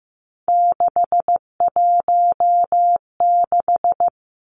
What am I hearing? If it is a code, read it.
Morse code, heard as a keyed tone, about 15 wpm, 700 Hz. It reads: 616
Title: Three Challenging Letter-Number Combinations 15wpm